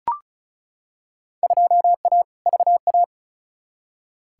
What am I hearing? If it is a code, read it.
Morse code, heard as a keyed tone, about 34 wpm, 700 Hz. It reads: 2A VA
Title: Field Day Exchanges - 35wpm